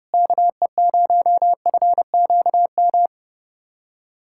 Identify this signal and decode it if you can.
Morse code, heard as a keyed tone, about 30 wpm, 700 Hz. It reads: KE0FQM
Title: USA Callsign DE Callsign 30wpm